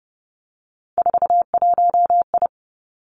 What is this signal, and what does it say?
Morse code, heard as a keyed tone, about 30 wpm, 700 Hz. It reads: 41I
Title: Three Challenging Letter-Number Combinations 30wpm